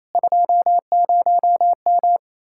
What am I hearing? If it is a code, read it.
Morse code, heard as a keyed tone, about 36 wpm, 700 Hz. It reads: 20M